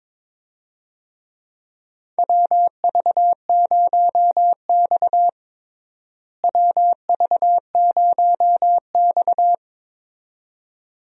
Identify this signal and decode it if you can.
Morse code, heard as a keyed tone, about 22 wpm, 700 Hz. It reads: WV0X WV0X
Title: USA Call Signs - 4 Characters - Repeated 2x - 22wpm